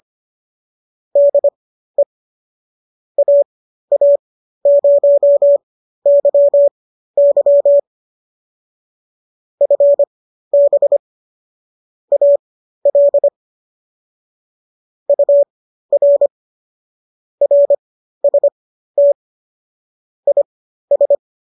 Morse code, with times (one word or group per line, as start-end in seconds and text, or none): 1.2-2.0 DE
3.2-7.8 AA0YY
9.6-11.0 FB
12.1-13.3 AL
15.1-16.3 UR
17.4-19.1 RST
20.3-21.2 IS